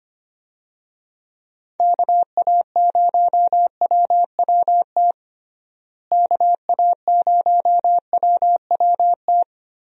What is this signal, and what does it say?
Morse code, heard as a keyed tone, about 25 wpm, 700 Hz. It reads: KA0WWT KA0WWT
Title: USA Call Signs - 6 Characters - Repeated 3x - 25wpm